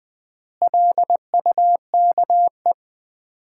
Morse code, heard as a keyed tone, about 21 wpm, 700 Hz. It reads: LUKE
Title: Common QSO Phrases - 4x Word-Spacing - 20wpm